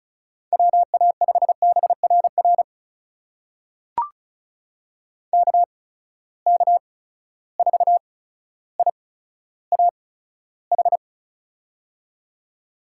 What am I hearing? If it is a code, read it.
Morse code, heard as a keyed tone, about 36 wpm, spaced as for 15 wpm, 700 Hz. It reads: WA5BRR KK4IAH
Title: USA Call Signs - 6 Characters - 8x Character Spacing - 35wpm